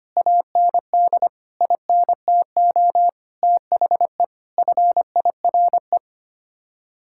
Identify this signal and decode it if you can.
Morse code, heard as a keyed tone, about 25 wpm, 700 Hz. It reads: AND INTO THE FIRE
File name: Sets of 9 Words - 25wpm